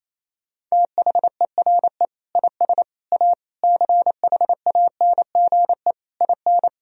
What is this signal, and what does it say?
Morse code, heard as a keyed tone, about 27 wpm, 700 Hz. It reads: THERE IS A CHANGE IN
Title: Sentences from Top 1000 Words - Review Part 1 - 28wpm